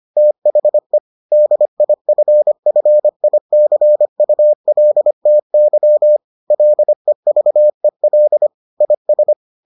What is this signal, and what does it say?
Morse code, heard as a keyed tone, about 25 wpm, 600 Hz. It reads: THE DIFFICULTY LEVEL IS